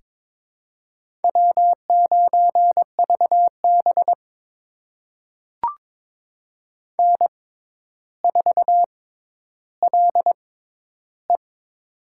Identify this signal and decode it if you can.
Morse code, heard as a keyed tone, about 22 wpm, spaced as for 11 wpm, 700 Hz. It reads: W9VB N4LE
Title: USA Call Signs - 4 Characters - 6x Character Spacing - 22wpm